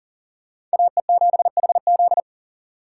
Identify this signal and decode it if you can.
Morse code, heard as a keyed tone, about 40 wpm, 700 Hz. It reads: AE7HZ
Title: USA Call Signs - 5 Characters - 3x Character Spacing - 40wpm